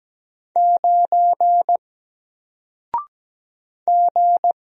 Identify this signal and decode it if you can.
Morse code, heard as a keyed tone, about 17 wpm, 700 Hz. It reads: 9 G
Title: Single Letter-Number 17wpm